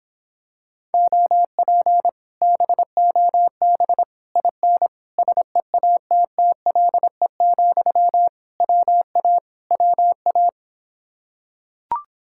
Morse code, heard as a keyed tone, about 26 wpm, 700 Hz. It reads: OP BOB IN SEATTLE, WA WA